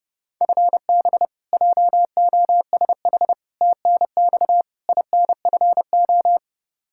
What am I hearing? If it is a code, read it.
Morse code, heard as a keyed tone, about 32 wpm, 700 Hz. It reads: FB JOSH TNX INFO
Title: Practice QSO - Exchange 3 - More Info - 30wpm